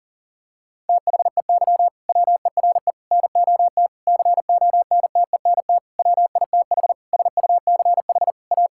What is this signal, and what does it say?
Morse code, heard as a keyed tone, about 40 wpm, 700 Hz. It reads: THEY WERE NOT CONTENT WITH SUCH A
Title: Sentences from Top 1000 Words - Review Part 6 - 40wpm